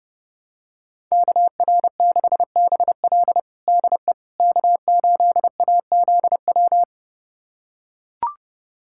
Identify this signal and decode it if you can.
Morse code, heard as a keyed tone, about 30 wpm, 700 Hz. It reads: KR6BL DE K8AZW
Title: USA Callsign DE Callsign - 5 Characters - 30wpm